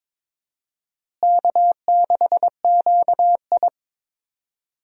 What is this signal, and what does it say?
Morse code, heard as a keyed tone, about 21 wpm, 700 Hz. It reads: K6QI
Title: USA Call Signs - 4 Characters - Repeated 3x - 22wpm